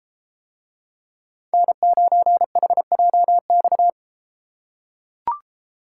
Morse code, heard as a keyed tone, about 32 wpm, 700 Hz. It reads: N9HJX